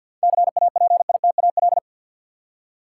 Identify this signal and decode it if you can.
Morse code, heard as a keyed tone, about 51 wpm, 700 Hz. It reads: CAPITAL
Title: Seven Letter Words - 50wpm